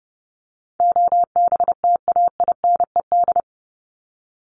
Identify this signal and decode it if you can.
Morse code, heard as a keyed tone, about 30 wpm, 700 Hz. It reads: OBTAINED
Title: ED Suffix - Top 100 Words with Base Words 30wpm